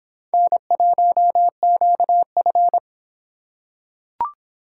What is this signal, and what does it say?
Morse code, heard as a keyed tone, about 24 wpm, 700 Hz. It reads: N1QF